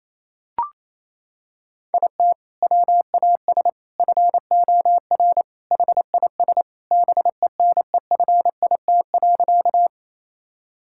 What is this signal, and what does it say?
Morse code, heard as a keyed tone, about 28 wpm, 700 Hz. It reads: IT WAS FOR HIS BENEFIT.
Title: Sentences from Top 1000 Words - Review Part 6 - 28wpm